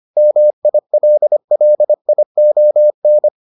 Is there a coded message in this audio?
Yes: Morse code from 0.2 to 3.3 s, heard as a keyed tone, about 25 wpm, 600 Hz. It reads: MILLION